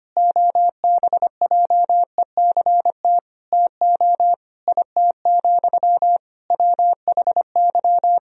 Morse code, heard as a keyed tone, about 25 wpm, 700 Hz. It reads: OBJECT TO IT, WHY
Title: Sentences from Top 1000 Words - Review Part 3 - 25wpm